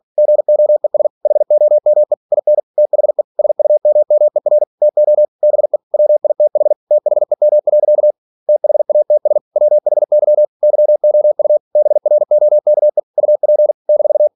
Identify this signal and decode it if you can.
Morse code, heard as a keyed tone, about 46 wpm, 600 Hz. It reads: GOES HOME IN THE SUMMER TO BE WITH THEM. THATS WHY YOU BROKE UP -
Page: Never 47wpm